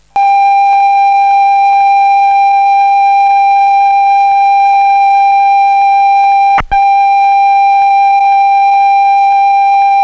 Computer data on cassette.